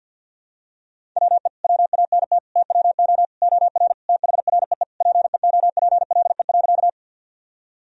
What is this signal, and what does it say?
Morse code, heard as a keyed tone, about 51 wpm, 700 Hz. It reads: WE WANT TWO OR THREE PEOPLE.
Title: Sentences from Top 1000 Words - Review Part 1 - 50wpm